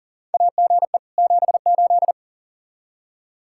Morse code, heard as a keyed tone, about 39 wpm, 700 Hz. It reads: AGE 78
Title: K1USN SST QSO Practice 40wpm